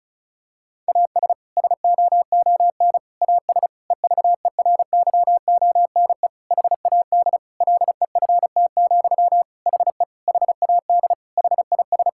Morse code, heard as a keyed tone, about 36 wpm, 700 Hz. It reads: AS SOON AS EVERYONE HAD LEFT, HE HAD HIS
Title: Sentences from Top 1000 Words - Review Part 5 - 35wpm